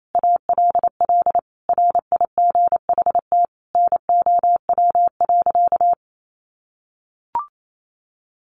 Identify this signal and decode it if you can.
Morse code, heard as a keyed tone, about 28 wpm, 700 Hz. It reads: ALL RIGHT NOW.
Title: Sentences from Top 700 Words 28wpm